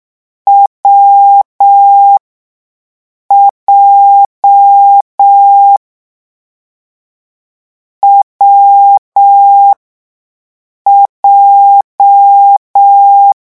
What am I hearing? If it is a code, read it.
Morse code, heard as a keyed tone, about 6 wpm, spaced as for 5 wpm, 800 Hz. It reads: WJ WJ
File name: morse mp3 / lecon7-vitesse 6